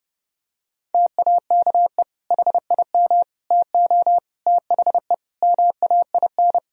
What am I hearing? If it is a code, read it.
Morse code, heard as a keyed tone, about 32 wpm, 700 Hz. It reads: TAKE HIM TO THE MAIN
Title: Sentences from Top 1000 Words - Review Part 4 - 30wpm